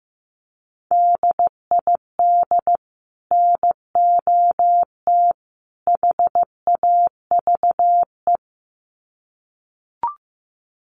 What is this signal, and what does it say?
Morse code, heard as a keyed tone, about 15 wpm, 700 Hz. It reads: DID NOT HAVE
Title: Sets of 3 Words 15wpm